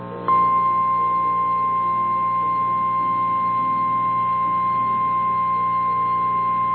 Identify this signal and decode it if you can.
Emergency Alert System (SAME) tones (the 1050 Hz attention tone). It is heard from 0.8 to 6.2 s.
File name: eas / KNCYKBIE / KNCYKBIE